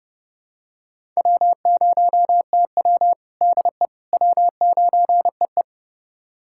Morse code, heard as a keyed tone, about 30 wpm, 700 Hz. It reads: W0TW DE W9EE
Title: USA Callsign DE Callsign - 4 Characters - 30wpm